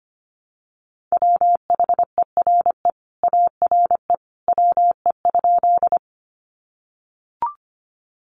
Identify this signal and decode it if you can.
Morse code, heard as a keyed tone, about 25 wpm, 700 Hz. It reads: WHERE ARE WE?
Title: Sentences from Top 200 Words 25wpm